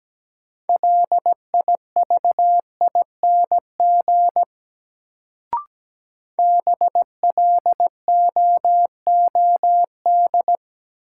Morse code, heard as a keyed tone, about 17 wpm, 700 Hz. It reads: LIVING BLOOD
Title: Sentences from Top 1100 Words - New Word Drill Alt - 17wpm